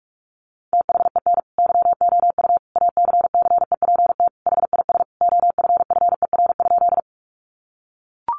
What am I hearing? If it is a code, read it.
Morse code, heard as a keyed tone, about 44 wpm, 700 Hz. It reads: THEN YOU ACCEPT HIS OFFER?
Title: Sentences from Top 1300 Words 45wpm